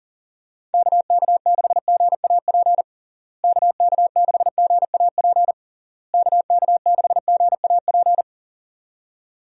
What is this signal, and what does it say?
Morse code, heard as a keyed tone, about 40 wpm, 700 Hz. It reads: KK6GAP KK6GAP KK6GAP
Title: USA Call Signs - 6 Characters - Repeated 3x - 40wpm